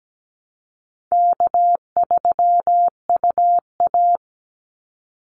Morse code, heard as a keyed tone, about 17 wpm, 700 Hz. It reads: K3UA
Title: Call Signs Repeated 2x - 17wpm